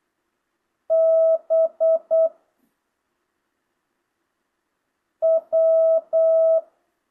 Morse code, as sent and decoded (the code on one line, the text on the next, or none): -... / .--
B W